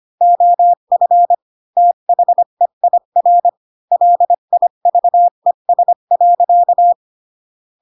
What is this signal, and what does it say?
Morse code, heard as a keyed tone, about 25 wpm, 700 Hz. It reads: OF THEIR LIVES.